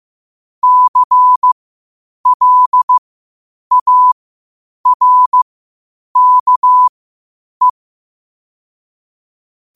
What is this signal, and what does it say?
Morse code, heard as a keyed tone, about 15 wpm, spaced as for 8 wpm, 1000 Hz. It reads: CLARKE
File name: qotdfiles / 8wpm